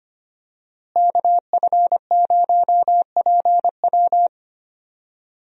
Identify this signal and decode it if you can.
Morse code, heard as a keyed tone, about 25 wpm, 700 Hz. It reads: KF0PW
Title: USA Call Signs - 5 Characters - Repeated 3x - 25wpm